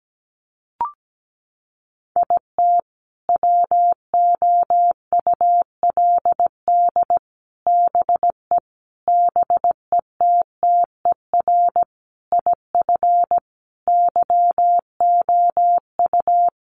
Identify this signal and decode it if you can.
Morse code, heard as a keyed tone, about 17 wpm, 700 Hz. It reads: IT WOULD BE BETTER IF YOU
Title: Sentences from Top 400 Words 17wpm